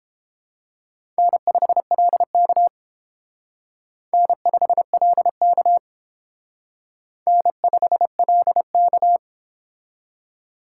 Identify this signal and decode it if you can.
Morse code, heard as a keyed tone, about 30 wpm, 700 Hz. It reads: N5LK N5LK N5LK